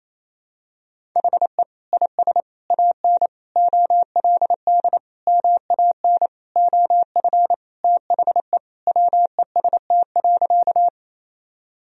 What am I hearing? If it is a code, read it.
Morse code, heard as a keyed tone, about 28 wpm, 700 Hz. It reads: HE IS AN OLD MAN OF THE WEST.
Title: Sentences from Top 1000 Words - Review Part 2 - 28wpm